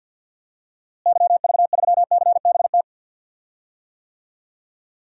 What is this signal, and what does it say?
Morse code, heard as a keyed tone, about 53 wpm, 700 Hz. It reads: YV3CBT
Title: DXCC-100 Call Signs - 6 Characters - Repeated 3x - 50wpm